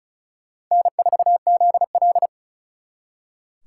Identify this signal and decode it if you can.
Morse code, heard as a keyed tone, about 45 wpm, 700 Hz. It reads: N4ZL